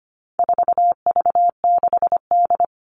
Morse code, heard as a keyed tone, about 25 wpm, 700 Hz. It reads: 4V6D
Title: All Cut Numbers - 4-Digit 25wpm